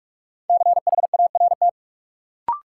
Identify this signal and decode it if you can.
Morse code, heard as a keyed tone, about 43 wpm, 700 Hz. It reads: CHART
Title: Sentences from Top 1300 Words - New Word Drill Alt - 45wpm